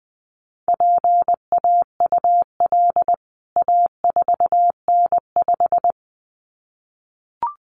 Morse code, heard as a keyed tone, about 20 wpm, 700 Hz. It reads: PAUL A4N5
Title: CWT Exchange 20wpm